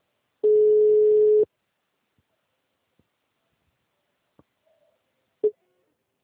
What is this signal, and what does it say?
Ringback tone (Europe, 425 Hz).